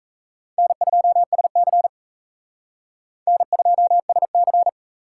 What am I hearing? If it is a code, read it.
Morse code, heard as a keyed tone, about 35 wpm, 700 Hz. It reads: N2SC N2SC
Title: Speed Racing - USA 1x2 Call Signs 28wpm